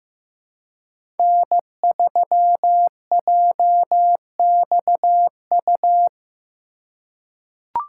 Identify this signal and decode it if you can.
Morse code, heard as a keyed tone, about 15 wpm, 700 Hz. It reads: N3JXU
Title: USA Call Signs - 3x Character Spacing - 15wpm